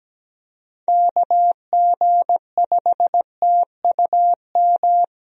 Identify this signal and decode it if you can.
Morse code, heard as a keyed tone, about 17 wpm, 700 Hz. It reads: KG5TUM
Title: USA Call Signs - 6 Characters - Repeated 3x - 17wpm